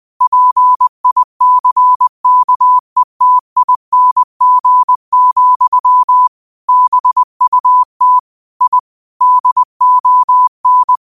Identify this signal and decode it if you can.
Morse code, heard as a keyed tone, about 20 wpm, 1000 Hz. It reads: PICKETING, BUT I DON
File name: qotdfiles / 20wpm